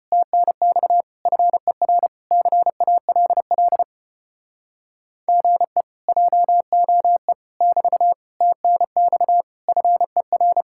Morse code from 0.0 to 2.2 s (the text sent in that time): TNX FER